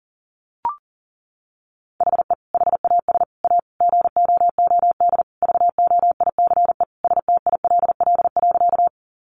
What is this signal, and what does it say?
Morse code, heard as a keyed tone, about 37 wpm, 700 Hz. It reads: HE HAS A GOOD VOICE STILL.
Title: Sentences from Top 500 Words 40wpm